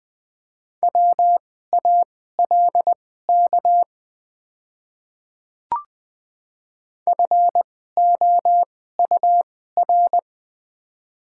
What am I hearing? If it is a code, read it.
Morse code, heard as a keyed tone, about 20 wpm, spaced as for 14 wpm, 700 Hz. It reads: WALK FOUR
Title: 2x Character Spacing - Top 300 Words - 20wpm